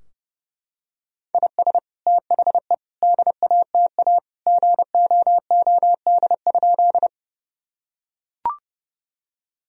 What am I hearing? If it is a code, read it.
Morse code, heard as a keyed tone, about 28 wpm, 700 Hz. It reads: IS THE DATA GOOD?
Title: Sentences from Top 1000 Words - Review Part 3 - 30wpm